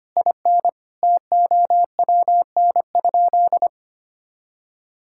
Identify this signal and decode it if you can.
Morse code, heard as a keyed tone, about 25 wpm, 700 Hz. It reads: IN TOWN?
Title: Sentences from Top 1000 Words - Review Part 6 - 25wpm